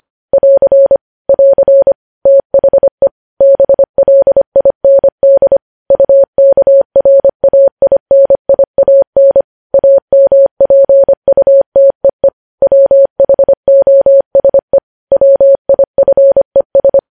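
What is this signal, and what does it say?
Morse code, heard as a keyed tone, about 25 wpm, 550 Hz. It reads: <AR> <AR> THE BLIND UKRAINIAN AMPUTEE WHOSE WIFES
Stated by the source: Morse code practice of news headlines at 25 wpm, opening with vvv.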